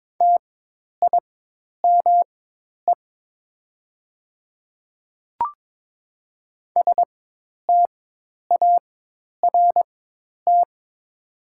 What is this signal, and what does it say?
Morse code, heard as a keyed tone, about 22 wpm, spaced as for 10 wpm, 700 Hz. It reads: TIME START
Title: 4x Character Spacing - Top 300 Words - 22wpm